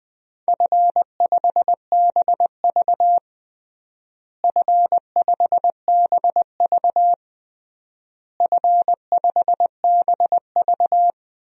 Morse code, heard as a keyed tone, about 20 wpm, 700 Hz. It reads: F5BV F5BV F5BV
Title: DXCC-100 Call Signs - 4 Characters - Repeated 3x - 20wpm